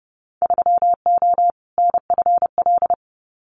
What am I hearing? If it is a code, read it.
Morse code, heard as a keyed tone, about 30 wpm, 700 Hz. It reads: 3O NFL
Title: Winter Field Day Exchanges 30wpm